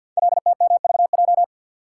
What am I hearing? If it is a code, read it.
Morse code, heard as a keyed tone, about 48 wpm, 700 Hz. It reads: LTMVJ
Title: Code Groups - 5 Letters - 50wpm